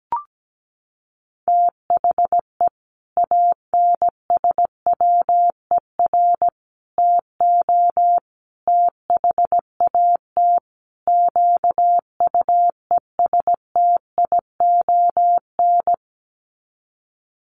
Morse code, heard as a keyed tone, about 17 wpm, 700 Hz. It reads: THE ANSWER TO THAT QUESTION
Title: Sets of 5 Words Encore2 17wpm